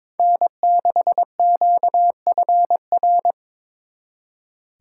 Morse code, heard as a keyed tone, about 26 wpm, 700 Hz. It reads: N6QFR
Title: Speed Racing - USA 1x3 Call Signs 22wpm